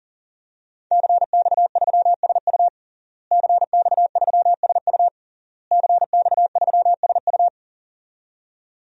Morse code, heard as a keyed tone, about 40 wpm, 700 Hz. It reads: CX3SU CX3SU CX3SU
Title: DXCC-100 Call Signs - 5 Characters - Repeated 3x - 40wpm